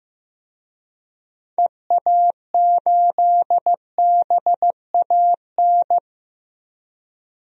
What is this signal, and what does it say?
Morse code, heard as a keyed tone, about 15 wpm, 700 Hz. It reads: EA8BAN
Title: DXCC-100 Call Signs - 6 Characters - 15wpm